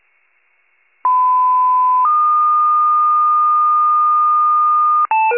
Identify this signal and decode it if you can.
Two-tone page, heard as a one-second tone, then a three-second tone, about 1010 then 1220 Hz. It is heard from 1.0 to 5.1 s.
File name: wcalert